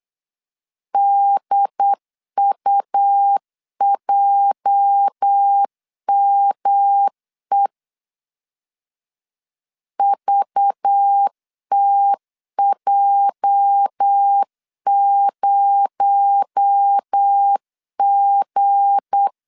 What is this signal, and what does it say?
Morse code, heard as a keyed tone, about 8 wpm, 800 Hz. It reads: DUJME VTJ0G
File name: LEZIONI / 8 YJ45 / 8 YJ 8wpm 3313